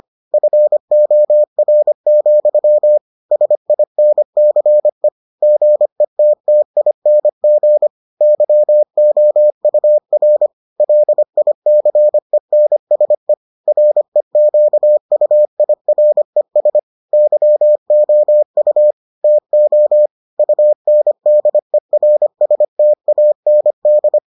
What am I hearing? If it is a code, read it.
Morse code, heard as a keyed tone, about 25 wpm, 600 Hz. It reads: FOR, SINCE GETTING YOUR LICENSE REQUIRES YOU TO UNDERSTAND